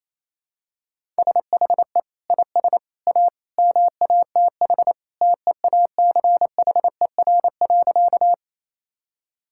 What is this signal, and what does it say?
Morse code, heard as a keyed tone, about 28 wpm, 700 Hz. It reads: SHE IS A MATH TEACHER.
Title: Sentences from Top 1000 Words - Review Part 6 - 28wpm